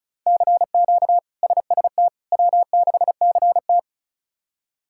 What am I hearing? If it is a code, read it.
Morse code, heard as a keyed tone, about 36 wpm, 700 Hz. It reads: CQ SST W6CT
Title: K1USN SST QSO Practice - 35wpm